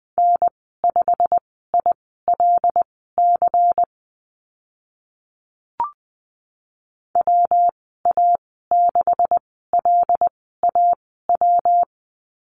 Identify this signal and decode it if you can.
Morse code, heard as a keyed tone, about 20 wpm, spaced as for 14 wpm, 700 Hz. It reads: N5ILC WA6LAW
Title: 140 Lesson 23g - Letter f - Call Signs 20wpm